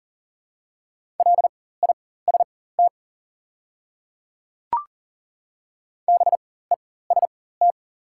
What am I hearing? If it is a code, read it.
Morse code, heard as a keyed tone, about 40 wpm, spaced as for 19 wpm, 690 Hz. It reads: LIST BEST
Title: 4x Character Spacing - Top 500 Words - 40wpm